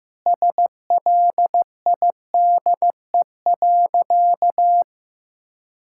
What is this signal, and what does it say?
Morse code, heard as a keyed tone, about 16 wpm, 700 Hz. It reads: SLIDE.
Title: Sentences from Top 1600 Words 15wpm